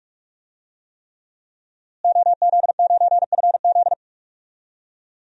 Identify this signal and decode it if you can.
Morse code, heard as a keyed tone, about 45 wpm, 700 Hz. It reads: OZ9FZ
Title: DXCC-100 Call Signs - 5 Characters - Repeated 3x - 45wpm